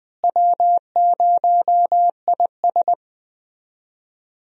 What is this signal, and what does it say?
Morse code, heard as a keyed tone, about 20 wpm, 700 Hz. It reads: W0IS
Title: USA Call Signs - 4 Characters - Repeated 3x - 20wpm